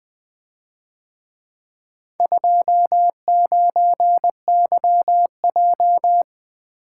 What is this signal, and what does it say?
Morse code, heard as a keyed tone, about 24 wpm, 700 Hz. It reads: 29YJ
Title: Speed-Racing - Four Letter-Number Combinations - 20wpm